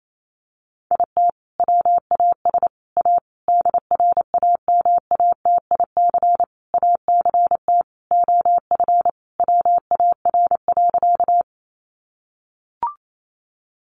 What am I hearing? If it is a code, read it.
Morse code, heard as a keyed tone, about 28 wpm, 700 Hz. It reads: IT WAS A DRAMATIC ACT OF WAR.
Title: Sentences from Top 1300 Words 28wpm